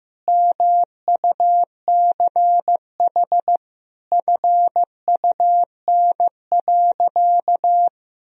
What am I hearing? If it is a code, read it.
Morse code, heard as a keyed tone, about 15 wpm, 700 Hz. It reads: MUCH FUN.
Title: Sentences from Top 1000 Words - Review Part 4 - 15wpm